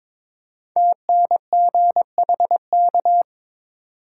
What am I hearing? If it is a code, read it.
Morse code, heard as a keyed tone, about 22 wpm, 700 Hz. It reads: TNGHK
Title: Code Groups - 5 Letter:Numbers - 22wpm